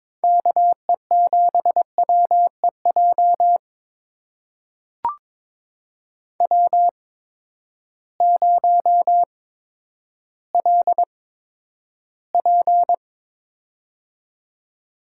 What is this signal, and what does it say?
Morse code, heard as a keyed tone, about 22 wpm, spaced as for 11 wpm, 700 Hz. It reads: KE7WEJ W0LP
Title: USA Call Signs - 8x Character Spacing - 22wpm